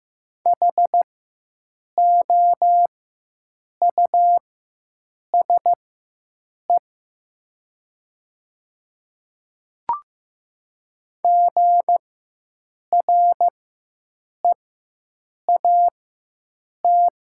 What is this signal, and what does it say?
Morse code, heard as a keyed tone, about 15 wpm, spaced as for 7 wpm, 700 Hz. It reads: HOUSE GREAT
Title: 4x Character Spacing - Top 200 Words - 15wpm